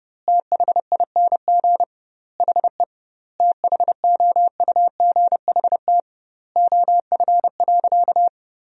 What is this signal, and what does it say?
Morse code, heard as a keyed tone, about 32 wpm, 700 Hz. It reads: THING HE THOUGHT OF.
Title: 2x Word-Spacing - Sentences from Top 300 Words - 30wpm